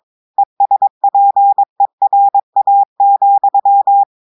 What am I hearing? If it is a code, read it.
Morse code, heard as a keyed tone, about 22 wpm, 800 Hz. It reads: ESPERA,
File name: audios / portugais / portugais texte en clair 1 vitesse 22 mots minutes